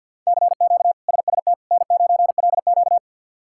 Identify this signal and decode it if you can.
Morse code, heard as a keyed tone, about 52 wpm, 700 Hz. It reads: CQ SST N9LQ
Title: K1USN SST QSO Practice - 50wpm